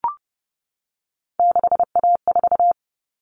Morse code, heard as a keyed tone, about 30 wpm, 700 Hz. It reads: 6A4
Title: All Cut Numbers - 3-Digit 30wpm